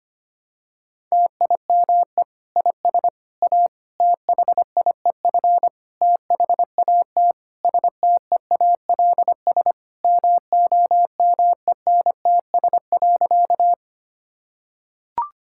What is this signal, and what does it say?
Morse code, heard as a keyed tone, about 25 wpm, 700 Hz. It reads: TIME IS A THIEF THAT STEALS MOMENTS.
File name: practice / 25wpm / Difficult Sentences with Hints - 25wpm